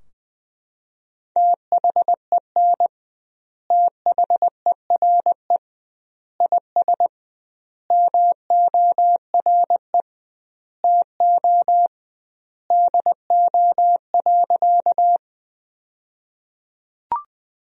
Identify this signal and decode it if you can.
Morse code, heard as a keyed tone, about 20 wpm, 700 Hz. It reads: THEN THERE IS MORE TO DO.
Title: 2x Word-Spacing - Sentences from Top 100 Words - 20wpm